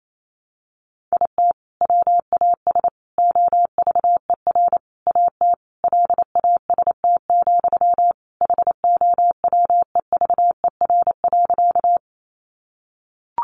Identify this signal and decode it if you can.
Morse code, heard as a keyed tone, about 28 wpm, 700 Hz. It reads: IT WAS OVER AT LAST, HOWEVER.
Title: Sentences from Top 600 Words 28wpm